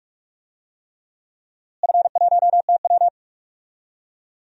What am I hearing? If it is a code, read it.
Morse code, heard as a keyed tone, about 47 wpm, 700 Hz. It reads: F1TW
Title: DXCC-100 Call Signs - 4 Characters - Repeated 3x - 45wpm